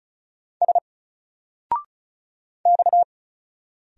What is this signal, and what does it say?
Morse code, heard as a keyed tone, about 36 wpm, 700 Hz. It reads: S X
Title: Single Letter-Number Alt 35wpm